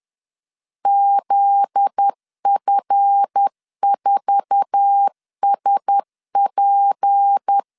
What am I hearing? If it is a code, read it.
Morse code, heard as a keyed tone, about 10 wpm, 800 Hz. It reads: ZF4SP
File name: LEZIONI / 10 SIGNS / 10C DOT 10wpm 2940